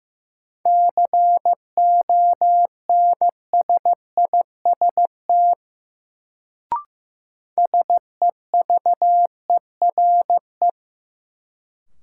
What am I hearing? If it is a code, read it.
Morse code, heard as a keyed tone, about 15 wpm, 700 Hz. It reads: CONSIST SEVERE
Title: Sentences from Top 1400 Words - New Word Drill - 15wpm